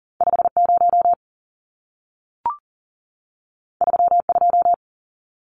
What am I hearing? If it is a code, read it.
Morse code, heard as a keyed tone, about 40 wpm, 700 Hz. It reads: 50 32
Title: Numbers - 2-Digit 40wpm